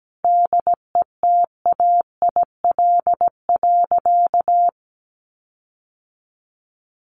Morse code, heard as a keyed tone, about 17 wpm, 700 Hz. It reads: DETAIL.
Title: Sentences from Top 1500 Words 17wpm